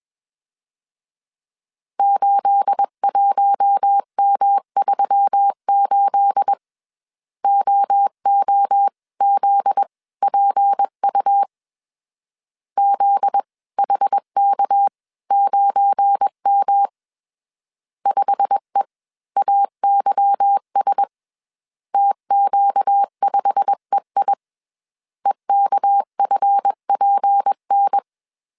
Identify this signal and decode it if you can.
Morse code, heard as a keyed tone, about 21 wpm, spaced as for 18 wpm, 800 Hz. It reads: 81M38 OOZPU ZHK9M 5EAYS TQ5EI EKFPN